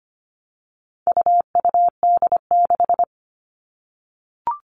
Morse code, heard as a keyed tone, about 25 wpm, 700 Hz. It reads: UUD6
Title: All Cut Numbers - 4-Digit 25wpm